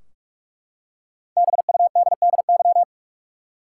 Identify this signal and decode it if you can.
Morse code, heard as a keyed tone, about 43 wpm, 700 Hz. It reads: BUDDY
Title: Five Letter Words - 45wpm